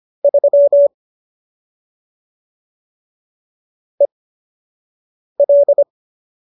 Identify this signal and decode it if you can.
Morse code, heard as a keyed tone, about 25 wpm, spaced as for 6 wpm, 570 Hz. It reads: 3 EL
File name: qso002 06